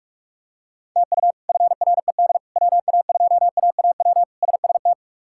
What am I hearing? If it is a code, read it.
Morse code, heard as a keyed tone, about 46 wpm, 700 Hz. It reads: TU FRED WA2AAW SST
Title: K1USN SST QSO Practice - 45wpm